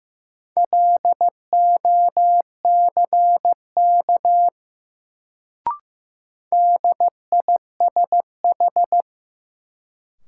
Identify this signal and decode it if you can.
Morse code, heard as a keyed tone, about 15 wpm, 700 Hz. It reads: LOCK DISH
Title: Sentences from Top 1300 Words - New Word Drill Alt - 15wpm